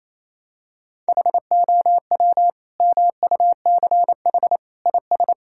Morse code, heard as a keyed tone, about 28 wpm, 700 Hz. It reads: HOW MUCH IS
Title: Sentences from Top 1000 Words - Review Part 5 - 28wpm